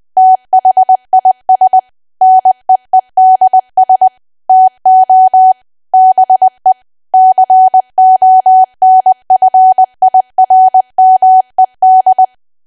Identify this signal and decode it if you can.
Morse code, heard as a keyed tone, about 20 wpm, 750 Hz. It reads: THIS NEEDS TO BE CONFIRMED
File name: cwa / W1AW-Level3 / 20WPM